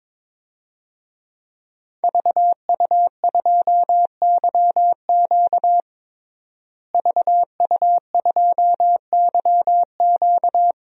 Morse code, heard as a keyed tone, about 22 wpm, 700 Hz. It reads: VU2YQ VU2YQ
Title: DXCC-100 Call Signs - 5 Characters - Repeated 3x - 22wpm